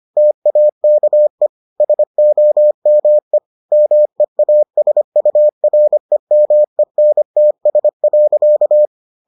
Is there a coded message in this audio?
Yes: Morse code from 0.2 to 8.9 s, heard as a keyed tone, about 25 wpm, 600 Hz. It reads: TAKE SOME MEASUREMENTS.